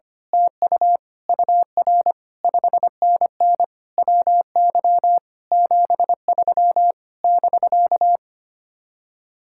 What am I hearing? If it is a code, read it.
Morse code, heard as a keyed tone, about 24 wpm, 700 Hz. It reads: TU UR 5NN WY 73 <BK>
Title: Speed-Racing Common QSO Phrases - 25wpm